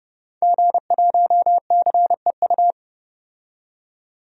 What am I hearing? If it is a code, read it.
Morse code, heard as a keyed tone, about 32 wpm, 700 Hz. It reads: G1CEU
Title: DXCC-100 Call Signs - 5 Characters - 30wpm